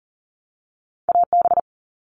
Morse code, heard as a keyed tone, about 40 wpm, 700 Hz. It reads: AB
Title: Canadian Provinces and Territories - Repeated 3x - 40wpm